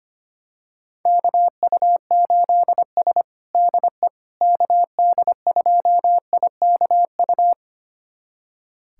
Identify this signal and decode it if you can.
Morse code, heard as a keyed tone, about 25 wpm, 700 Hz. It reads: KU8S DE KD2IKU
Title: USA Callsign DE Callsign 25wpm